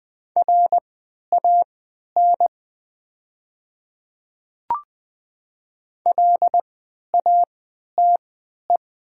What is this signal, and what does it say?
Morse code, heard as a keyed tone, about 20 wpm, spaced as for 11 wpm, 700 Hz. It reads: RAN LATE
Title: 3x Character Spacing - Top 500 Words - 20wpm